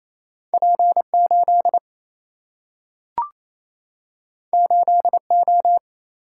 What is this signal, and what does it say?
Morse code, heard as a keyed tone, about 28 wpm, 700 Hz. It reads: P8 8O
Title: Single-Letter Single-Number 28wpm